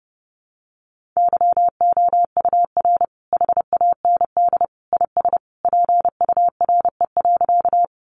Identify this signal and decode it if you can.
Morse code, heard as a keyed tone, about 28 wpm, 700 Hz. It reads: YOUR HAND IS PURE.
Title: Sentences from Top 1500 Words 30wpm